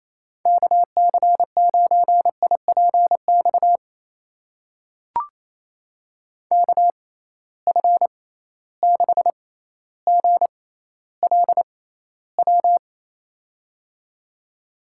Morse code, heard as a keyed tone, about 27 wpm, spaced as for 14 wpm, 700 Hz. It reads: KC9IPX KF6GLW
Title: USA Call Signs - 6 Characters - 6x Character Spacing - 28wpm